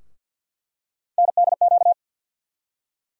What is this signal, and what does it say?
Morse code, heard as a keyed tone, about 50 wpm, 700 Hz. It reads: NDQ
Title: USA Call Sign Postfixes - 3 Letters 50wpm